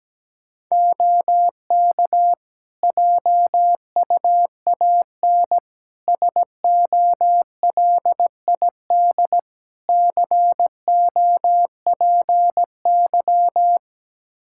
Morse code, heard as a keyed tone, about 17 wpm, 700 Hz. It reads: OK JUAN SOLID COPY
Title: K1USN SST QSO Practice 17wpm